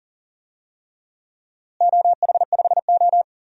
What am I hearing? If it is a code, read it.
Morse code, heard as a keyed tone, about 37 wpm, 700 Hz. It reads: OH5O
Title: DXCC-100 Call Signs - 4 Characters - Repeated 3x - 40wpm